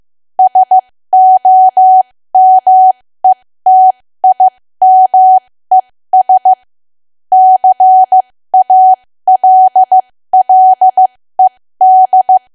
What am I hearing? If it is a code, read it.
Morse code, heard as a keyed tone, about 15 wpm, spaced as for 13 wpm, 750 Hz. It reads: SOMETIMES CALLED
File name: cwa / W1AW-Level2 / 13WPM